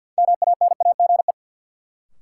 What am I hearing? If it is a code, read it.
Morse code, heard as a keyed tone, about 50 wpm, 700 Hz. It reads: MANAGE
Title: Sentences from Top 1400 Words - New Word Drill Alt - 50wpm